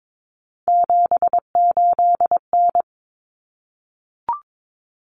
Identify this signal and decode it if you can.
Morse code, heard as a keyed tone, about 22 wpm, 700 Hz. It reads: 78N
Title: Common Cut Numbers - 3-Digit 22wpm